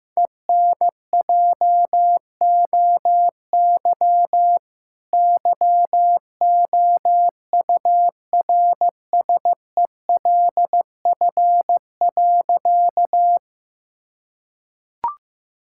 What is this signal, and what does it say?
Morse code, heard as a keyed tone, about 15 wpm, 700 Hz. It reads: ENJOY YOURSELF.
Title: Sentences from Top 1000 Words - Review Part 5 - 15wpm